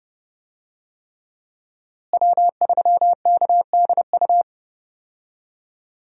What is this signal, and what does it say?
Morse code, heard as a keyed tone, about 30 wpm, 700 Hz. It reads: W3KDU
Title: USA Call Signs - 5 Characters - Repeated 3x - 30wpm